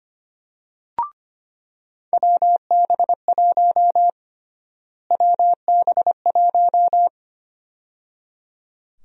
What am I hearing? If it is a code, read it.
Morse code, heard as a keyed tone, about 25 wpm, 700 Hz. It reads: WB1 WB1
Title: USA Call Sign Prefixes - Repeated 2x - 25wpm